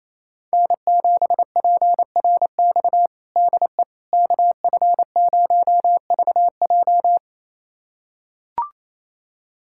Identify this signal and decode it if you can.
Morse code, heard as a keyed tone, about 28 wpm, 700 Hz. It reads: N7PRX DE KF0VJ
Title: USA Callsign DE Callsign - 5 Characters - 28wpm